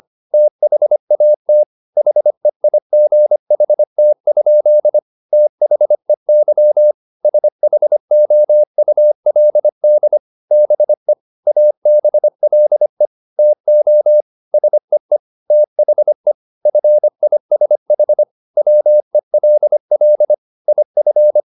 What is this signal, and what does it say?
Morse code, heard as a keyed tone, about 25 wpm, 600 Hz. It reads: THAT HEIGHT? THEY SHOULD BE ABLE TO SEE THE FISH WELL IF